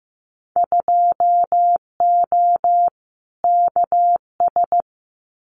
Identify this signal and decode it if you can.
Morse code, heard as a keyed tone, about 15 wpm, 700 Hz. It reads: 2O KS
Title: Winter Field Day Exchanges Encore 15wpm